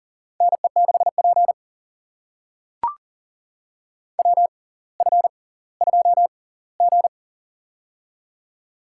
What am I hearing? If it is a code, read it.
Morse code, heard as a keyed tone, about 40 wpm, spaced as for 21 wpm, 700 Hz. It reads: NE6P WF2G
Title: USA Call Signs - 4 Characters - 6x Character Spacing - 40wpm